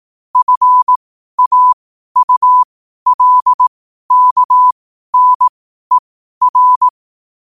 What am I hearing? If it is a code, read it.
Morse code, heard as a keyed tone, about 18 wpm, spaced as for 12 wpm, 1000 Hz. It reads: FAULKNER
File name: qotdfiles / 12wpm